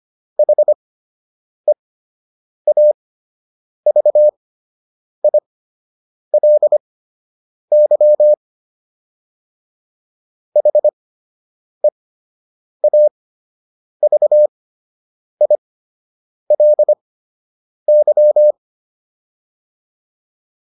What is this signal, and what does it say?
Morse code, heard as a keyed tone, about 25 wpm, spaced as for 8 wpm, 600 Hz. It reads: HEAVILY HEAVILY